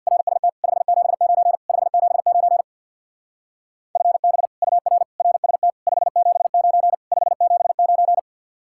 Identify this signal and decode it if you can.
Morse code, heard as a keyed tone, about 65 wpm, 700 Hz. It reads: RST 579 579 FB UR RST 579 579